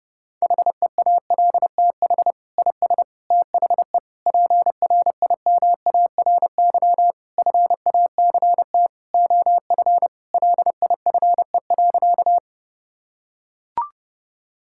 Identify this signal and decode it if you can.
Morse code, heard as a keyed tone, about 29 wpm, 700 Hz. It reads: HEALTH IS THE PRIMARY FACT OF LIFE.
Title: Sentences from Top 1000 Words - Review Part 5 - 30wpm